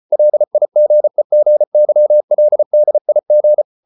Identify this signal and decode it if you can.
Morse code, heard as a keyed tone, about 34 wpm, 600 Hz. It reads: LIGEGYLDIG
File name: Ferie QRQ 34wpm